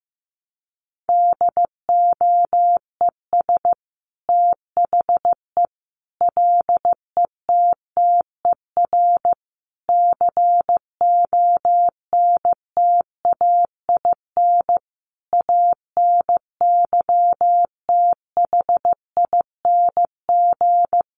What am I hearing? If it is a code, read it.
Morse code, heard as a keyed tone, about 15 wpm, 700 Hz. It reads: DOES THE LETTER CONTAIN ANYTHING
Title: Sentences from Top 1100 Words 15wpm